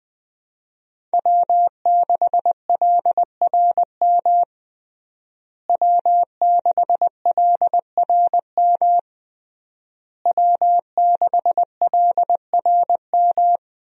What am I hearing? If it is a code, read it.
Morse code, heard as a keyed tone, about 21 wpm, 700 Hz. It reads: W6LRM W6LRM W6LRM
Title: USA Call Signs - 5 Characters - Repeated 3x - 20wpm